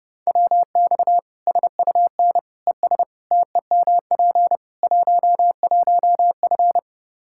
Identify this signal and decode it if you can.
Morse code, heard as a keyed tone, about 30 wpm, 700 Hz. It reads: WX SUN ES TEMP 11F
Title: Practice QSO - Exchange 2 - Info - 30wpm